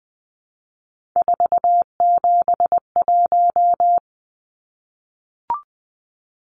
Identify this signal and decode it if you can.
Morse code, heard as a keyed tone, about 20 wpm, 700 Hz. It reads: 471
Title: Numbers - 3-Digit 20wpm